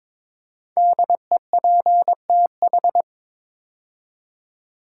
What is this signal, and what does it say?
Morse code, heard as a keyed tone, about 22 wpm, 700 Hz. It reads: DEPTH
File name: 5 Letter Words - First 3 Letters Given - 22wpm